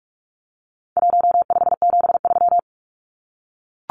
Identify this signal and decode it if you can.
Morse code, heard as a keyed tone, about 44 wpm, 700 Hz. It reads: 1573
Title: Numbers - Contest Style - 4-Digit 45wpm